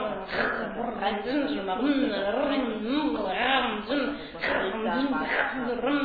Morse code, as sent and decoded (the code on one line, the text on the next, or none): none